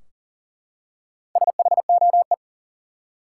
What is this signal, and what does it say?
Morse code, heard as a keyed tone, about 40 wpm, 700 Hz. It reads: SHOE
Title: Four Letter Words - 40wpm